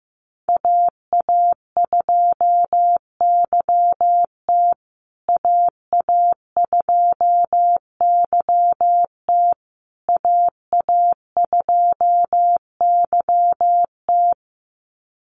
Morse code, heard as a keyed tone, about 15 wpm, 700 Hz. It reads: AA2YT AA2YT AA2YT
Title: USA Call Signs Repeated 3x 15wpm